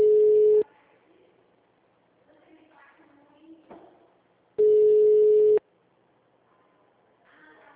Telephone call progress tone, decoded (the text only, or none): ringback tone